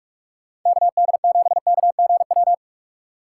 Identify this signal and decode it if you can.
Morse code, heard as a keyed tone, about 43 wpm, 690 Hz. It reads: KD7KGW
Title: USA Call Signs - 4x Character Spacing - 45wpm